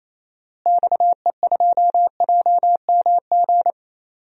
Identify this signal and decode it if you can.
Morse code, heard as a keyed tone, about 28 wpm, 700 Hz. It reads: XE2JMG
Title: DXCC-100 Call Signs - 6 Characters - Repeated 2x - 28wpm